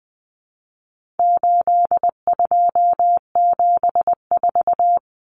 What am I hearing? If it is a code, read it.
Morse code, heard as a keyed tone, about 20 wpm, 700 Hz. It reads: 8274
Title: Numbers - 4-Digit 20wpm